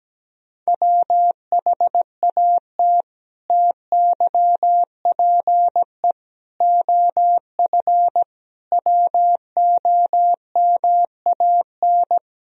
Morse code, heard as a keyed tone, about 17 wpm, 700 Hz. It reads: WHAT TYPE OF WOMAN
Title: Sentences from Top 1000 Words - Review Part 3 - 17wpm